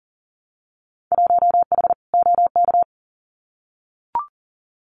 Morse code, heard as a keyed tone, about 38 wpm, 700 Hz. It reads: 1H OK
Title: Winter Field Day Exchanges 40wpm